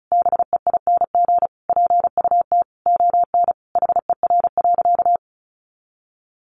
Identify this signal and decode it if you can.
Morse code, heard as a keyed tone, about 34 wpm, 700 Hz. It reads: BEING PUT ON HER.
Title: Sentences from Top 1300 Words 35wpm